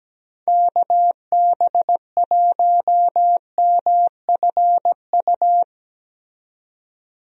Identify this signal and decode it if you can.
Morse code, heard as a keyed tone, about 17 wpm, 700 Hz. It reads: KB1MFU
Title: USA Call Signs - 6 Characters - 17wpm